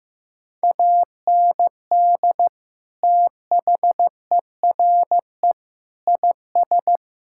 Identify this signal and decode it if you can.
Morse code, heard as a keyed tone, about 16 wpm, 700 Hz. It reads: AND THERE IS
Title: Sentences from Top 1600 Words 15wpm